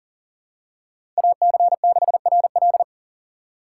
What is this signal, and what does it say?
Morse code, heard as a keyed tone, about 40 wpm, 700 Hz. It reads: AC6RL
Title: USA Call Signs - 5 Characters - 4x Character Spacing - 40wpm